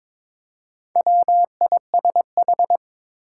Morse code, heard as a keyed tone, about 22 wpm, 700 Hz. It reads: WISH
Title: Four Letter Words - 22wpm